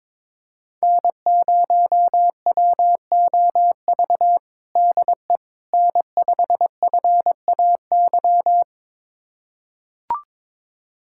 Morse code, heard as a keyed tone, about 22 wpm, 700 Hz. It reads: N0WOV DE N5FAY
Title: USA Callsign DE Callsign - 5 Characters - 22wpm